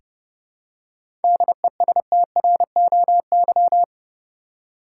Morse code, heard as a keyed tone, about 30 wpm, 700 Hz. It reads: DESTROY
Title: Seven Letter Words - 30wpm